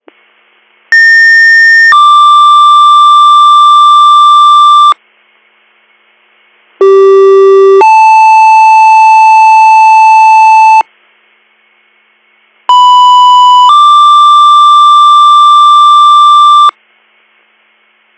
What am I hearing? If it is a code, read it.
Two-tone page, heard as a one-second tone, then a three-second tone, about 1800 then 1180 Hz, about 390 then 880 Hz, about 990 then 1180 Hz. It is heard from 0.9 to 16.7 s.